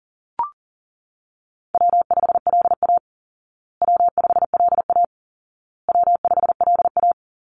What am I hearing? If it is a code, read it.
Morse code, heard as a keyed tone, about 40 wpm, 700 Hz. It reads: W5LA W5LA W5LA
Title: Call Signs Repeated 2x - 40wpm